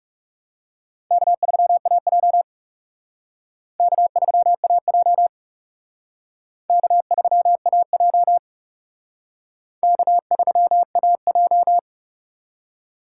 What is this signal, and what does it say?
Morse code, heard as a keyed tone, about 37 wpm, 700 Hz. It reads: K3AJ K3AJ K3AJ K3AJ
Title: Speed Racing - USA 1x2 Call Signs 30wpm